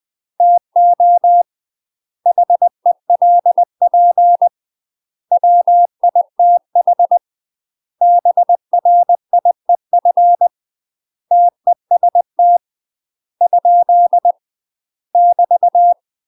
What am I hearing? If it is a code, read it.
Morse code, heard as a keyed tone, about 20 wpm, 700 Hz. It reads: TO HELP WITH BRIEF TEST ? <BT>
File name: UK MASTER QSOs TRAINING / 20WPM / UK MASTER QSOs 0034